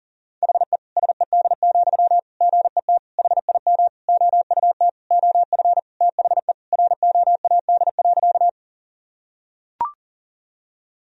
Morse code, heard as a keyed tone, about 40 wpm, 700 Hz. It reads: HE SED, GET HIM OUT OF THE ROAD.
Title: Sentences from Top 400 Words and Abbreviations 40wpm